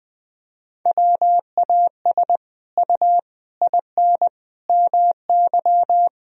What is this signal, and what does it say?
Morse code, heard as a keyed tone, about 19 wpm, 700 Hz. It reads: WAS U IN MY
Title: Sentences from Top 300 Words and Abbreviations 20wpm